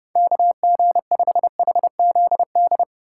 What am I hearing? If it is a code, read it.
Morse code, heard as a keyed tone, about 30 wpm, 700 Hz. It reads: KG5HZD
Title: USA Call Signs - 6 Characters - 4x Character Spacing - 30wpm